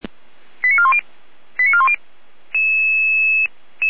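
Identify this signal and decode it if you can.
Five-tone selective call (ZVEI), heard as quick runs of five tones, tones of about 70 ms. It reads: 89410 89410
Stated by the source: ZVEI five-tone alert received by SDR, from the audio archive of the feuerwehr-orlamuende fire brigade.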